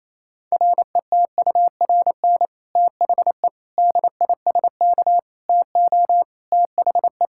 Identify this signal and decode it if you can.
Morse code, heard as a keyed tone, about 28 wpm, 700 Hz. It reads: RETURN THE DISK TO THE
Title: Sentences from Top 1000 Words - Review Part 6 - 28wpm